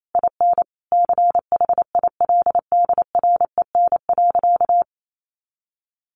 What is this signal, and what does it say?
Morse code, heard as a keyed tone, about 28 wpm, 700 Hz. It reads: IN CHILDREN.
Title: Sentences from Top 1100 Words 28wpm